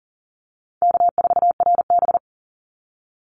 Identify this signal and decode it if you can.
Morse code, heard as a keyed tone, about 40 wpm, 700 Hz. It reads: K4RB
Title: Easy Call Signs Repeated 3x - 40wpm